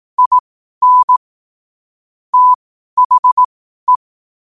Morse code, heard as a keyed tone, about 18 wpm, spaced as for 12 wpm, 1000 Hz. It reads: IN THE
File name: qotdfiles / 12wpm